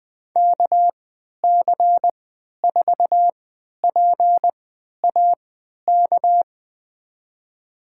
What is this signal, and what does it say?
Morse code, heard as a keyed tone, about 20 wpm, spaced as for 11 wpm, 700 Hz. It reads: KC4PAK
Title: USA Call Signs - 6 Characters - 3x Character Spacing - 20wpm